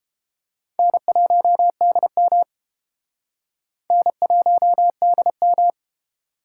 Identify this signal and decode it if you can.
Morse code, heard as a keyed tone, about 30 wpm, 700 Hz. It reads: N1DM N1DM